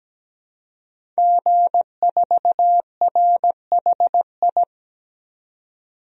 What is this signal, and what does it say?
Morse code, heard as a keyed tone, about 17 wpm, 700 Hz. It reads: G4RHI
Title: DXCC-100 Call Signs - 5 Characters - 17wpm